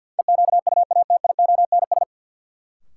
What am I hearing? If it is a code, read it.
Morse code, heard as a keyed tone, about 46 wpm, 700 Hz. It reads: EQUATIONS
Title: Sets of 5 Words - New Word Drill - 50wpm